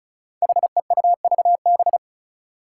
Morse code, heard as a keyed tone, about 34 wpm, 700 Hz. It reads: HEUVB
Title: Five Challenging Letter-Number Combinations 35wpm